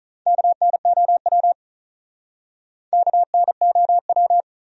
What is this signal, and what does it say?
Morse code, heard as a keyed tone, about 43 wpm, 700 Hz. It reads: KNOW KNOW
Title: Speed-Racing Top 100 Words 35wpm